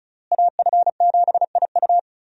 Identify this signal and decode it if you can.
Morse code, heard as a keyed tone, about 36 wpm, 700 Hz. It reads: AF7IU
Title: USA Call Signs - 5 Characters - Repeated 2x - 35wpm